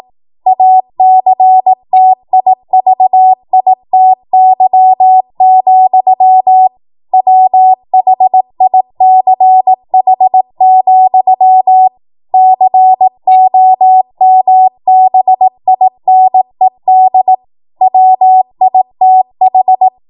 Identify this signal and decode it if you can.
Morse code, heard as a keyed tone, about 18 wpm, 750 Hz. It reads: ACTIVITY, WHICH, COMBINED WITH
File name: Morse / 18WPM